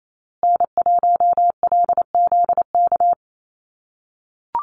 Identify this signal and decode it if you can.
Morse code, heard as a keyed tone, about 28 wpm, 700 Hz. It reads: N1LZK
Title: USA Call Signs Repeated 2x 28wpm